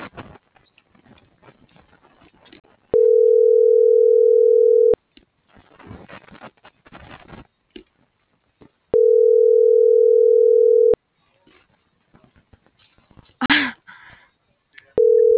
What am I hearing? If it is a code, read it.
Ringback tone (North America), 440 + 480 Hz.